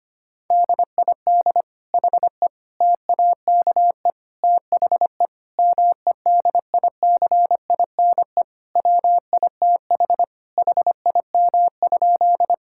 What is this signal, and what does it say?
Morse code, heard as a keyed tone, about 25 wpm, 700 Hz. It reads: DID HE TAKE THE MEDICINE WITH HIM?
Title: Sentences from Top 1000 Words - Review Part 6 - 25wpm